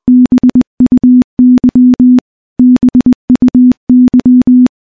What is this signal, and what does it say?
Morse code, heard as a keyed tone, about 20 wpm, 250 Hz. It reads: BUY BUY
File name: morse / morse-training-20WPM